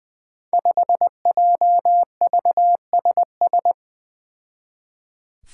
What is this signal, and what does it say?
Morse code, heard as a keyed tone, about 20 wpm, 700 Hz. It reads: 5JVSS
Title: Five Challenging Letter-Number Combinations 20wpm